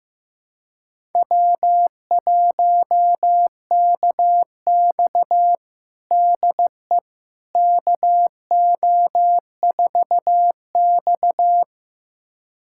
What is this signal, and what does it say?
Morse code, heard as a keyed tone, about 15 wpm, 700 Hz. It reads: W1KX DE KO4X
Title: USA Callsign DE Callsign - 4 Characters - 15wpm